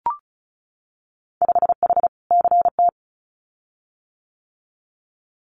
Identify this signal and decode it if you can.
Morse code, heard as a keyed tone, about 34 wpm, 700 Hz. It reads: 5H CT
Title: Winter Field Day Exchanges Encore 35wpm